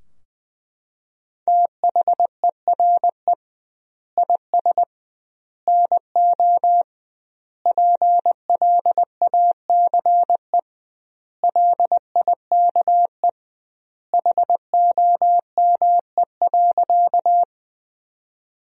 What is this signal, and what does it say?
Morse code, heard as a keyed tone, about 19 wpm, 700 Hz. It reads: THERE IS NO PLACE LIKE HOME.
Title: 2x Word-Spacing - Sentences from Top 200 Words - 20wpm